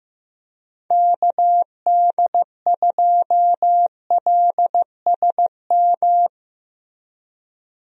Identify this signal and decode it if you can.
Morse code, heard as a keyed tone, about 15 wpm, 700 Hz. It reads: KD2LSM
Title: USA Call Signs - 6 Characters - Repeated 3x - 15wpm